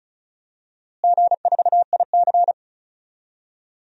Morse code, heard as a keyed tone, about 34 wpm, 700 Hz. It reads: G4IC
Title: DXCC-100 Call Signs - 4 Characters - Repeated 2x - 35wpm